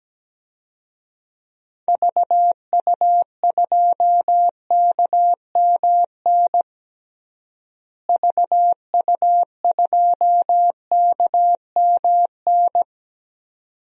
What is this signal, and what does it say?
Morse code, heard as a keyed tone, about 17 wpm, 700 Hz. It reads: VU2KMN VU2KMN
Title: DXCC-100 Call Signs - 6 Characters - Repeated 2x - 17wpm